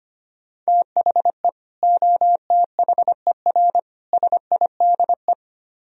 Morse code, heard as a keyed tone, about 25 wpm, 700 Hz. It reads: THE OTHER SIDE
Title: Speed-Racing - Sets of 3 Words 25wpm